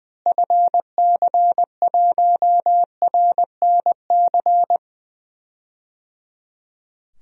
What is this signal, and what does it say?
Morse code, heard as a keyed tone, about 21 wpm, 700 Hz. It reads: FC1RNC
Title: DXCC-100 Call Signs - 6 Characters - Repeated 3x - 20wpm